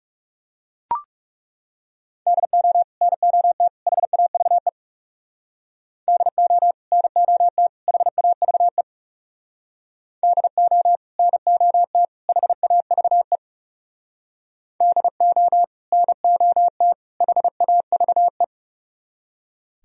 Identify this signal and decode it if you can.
Morse code, heard as a keyed tone, about 36 wpm, 700 Hz. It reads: DO NOT HAVE DO NOT HAVE DO NOT HAVE DO NOT HAVE
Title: Speed-Racing - Sets of 3 Words 30wpm